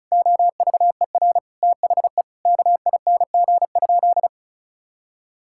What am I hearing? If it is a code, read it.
Morse code, heard as a keyed tone, about 34 wpm, 700 Hz. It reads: OVER THE KING?
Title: Sentences from Top 1000 Words - Review Part 6 - 35wpm